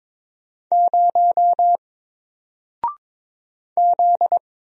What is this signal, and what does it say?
Morse code, heard as a keyed tone, about 22 wpm, 700 Hz. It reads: 0 Z
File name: Single-Letter-Number-22wpm